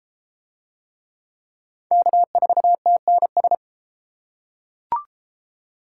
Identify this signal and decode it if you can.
Morse code, heard as a keyed tone, about 31 wpm, 700 Hz. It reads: K4TNS